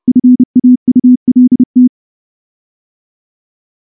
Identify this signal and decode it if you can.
Morse code, heard as a keyed tone, about 28 wpm, 250 Hz. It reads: FAULT
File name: morse / morse-training-30WPM